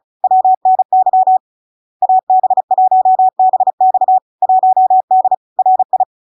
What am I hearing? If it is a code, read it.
Morse code, heard as a keyed tone, about 36 wpm, 750 Hz. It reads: WNY AB1BX 1D RI
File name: LARK-FD-2022-Log-in-CW-35wpm-750-Hz